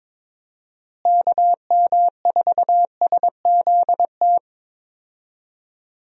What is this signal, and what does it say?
Morse code, heard as a keyed tone, about 22 wpm, 700 Hz. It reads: KM4SZT
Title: USA Call Signs - 6 Characters - Repeated 3x - 22wpm